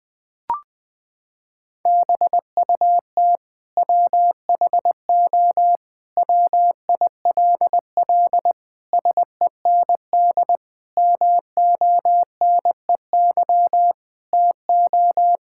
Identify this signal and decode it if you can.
Morse code, heard as a keyed tone, about 21 wpm, 700 Hz. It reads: BUT WHO WILL SEND MONEY TO
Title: Sentences from Top 1600 Words 20wpm